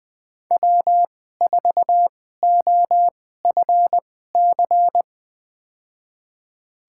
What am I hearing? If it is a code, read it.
Morse code, heard as a keyed tone, about 20 wpm, spaced as for 15 wpm, 700 Hz. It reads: W4OFC
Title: USA Call Signs - 5 Characters - 2x Character Spacing - 20wpm